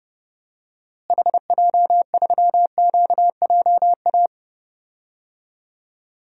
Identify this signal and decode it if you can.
Morse code, heard as a keyed tone, about 30 wpm, 700 Hz. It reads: HJ3QJA
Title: DXCC-100 Call Signs - 6 Characters - Repeated 2x - 30wpm